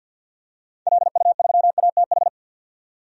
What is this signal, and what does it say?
Morse code, heard as a keyed tone, about 47 wpm, 700 Hz. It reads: LU3ATH
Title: DXCC-100 Call Signs - 6 Characters - 50wpm